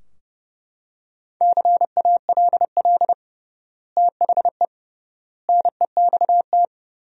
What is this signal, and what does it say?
Morse code, heard as a keyed tone, about 30 wpm, 700 Hz. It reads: CALL THE NEXT
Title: 3x Word-Spacing - Sentences from Top 300 Words - 30wpm